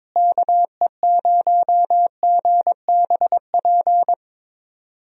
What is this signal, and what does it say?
Morse code, heard as a keyed tone, about 22 wpm, 700 Hz. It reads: KE0GBP
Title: USA Call Signs - 6 Characters - Repeated 3x - 22wpm